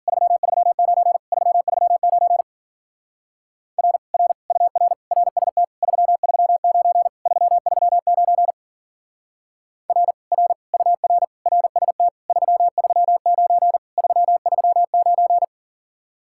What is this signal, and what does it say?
Morse code, heard as a keyed tone, about 52 wpm, 700 Hz. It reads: 339 339 R R UR RST 339 339 R R UR RST 339 339